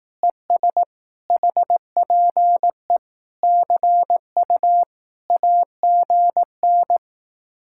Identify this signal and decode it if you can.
Morse code, heard as a keyed tone, about 18 wpm, 700 Hz. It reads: ES HPE CU AGN
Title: Speed-Racing Common QSO Phrases Encore - 15wpm